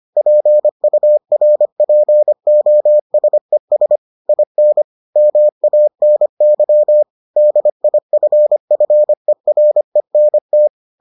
Morse code, heard as a keyed tone, about 25 wpm, 600 Hz. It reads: PURPOSES IN MANY DIFFERENT